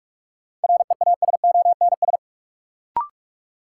Morse code, heard as a keyed tone, about 47 wpm, 700 Hz. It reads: REASONS
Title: Sets of 4 Words - New Word Drill - 45wpm